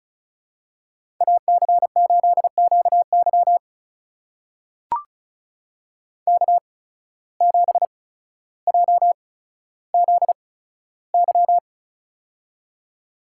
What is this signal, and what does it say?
Morse code, heard as a keyed tone, about 36 wpm, spaced as for 15 wpm, 700 Hz. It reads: AC8QY K7JZY
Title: USA Call Signs - 5 Characters - 8x Character Spacing - 35wpm